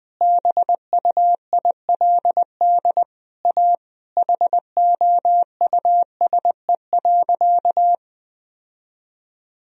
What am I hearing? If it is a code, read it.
Morse code, heard as a keyed tone, about 20 wpm, 700 Hz. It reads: BUILD A HOUSE.
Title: Sentences from Top 1000 Words - Review Part 4 - 20wpm